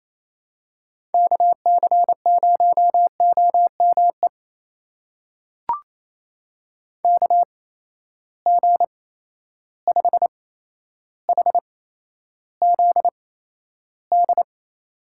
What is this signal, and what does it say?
Morse code, heard as a keyed tone, about 28 wpm, spaced as for 12 wpm, 700 Hz. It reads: KC0OME KG5HZD
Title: USA Call Signs - 6 Characters - 8x Character Spacing - 28wpm